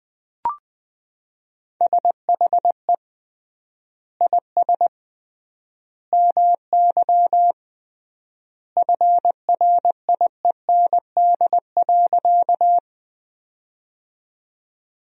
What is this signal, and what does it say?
Morse code, heard as a keyed tone, about 20 wpm, 700 Hz. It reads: SHE IS MY FRIEND.
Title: 3x Word-Spacing - Sentences from Top 300 Words - 20wpm